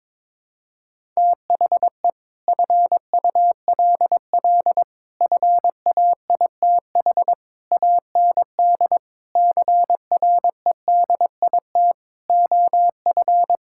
Morse code, heard as a keyed tone, about 22 wpm, 700 Hz. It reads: THE FULL FAITH AND CREDIT OF
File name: practice / 22wpm / Sets of 8 Words - 22wpm